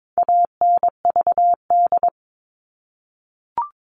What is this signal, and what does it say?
Morse code, heard as a keyed tone, about 22 wpm, 700 Hz. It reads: ANVD
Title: All Cut Numbers - 4-Digit 22wpm